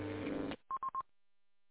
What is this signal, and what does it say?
There is no code.